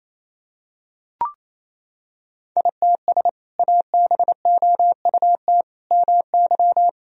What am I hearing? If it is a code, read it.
Morse code, heard as a keyed tone, about 28 wpm, 700 Hz. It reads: ITS ABOUT MY
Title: Sentences from Top 1000 Words - Review Part 1 - 28wpm